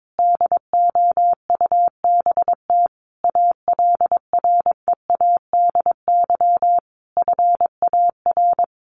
Morse code, heard as a keyed tone, about 22 wpm, 700 Hz. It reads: DOUBT ALREADY FAR
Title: Sentences from Top 1500 Words 22wpm